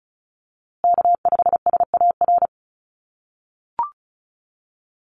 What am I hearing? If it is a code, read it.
Morse code, heard as a keyed tone, about 34 wpm, 700 Hz. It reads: K5SAR
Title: USA Call Signs Repeated 3x 35wpm